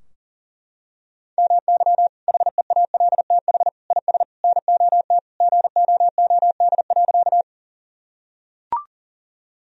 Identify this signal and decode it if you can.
Morse code, heard as a keyed tone, about 40 wpm, 700 Hz. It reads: MY HEALTH IS NOT GOOD.
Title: Sentences from Top 1000 Words - Review Part 3 - 40wpm